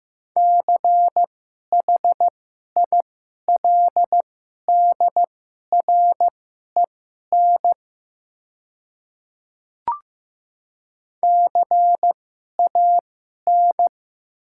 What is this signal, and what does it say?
Morse code, heard as a keyed tone, about 15 wpm, spaced as for 11 wpm, 700 Hz. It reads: CHILDREN CAN
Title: 2x Character Spacing - Top 300 Words - 15wpm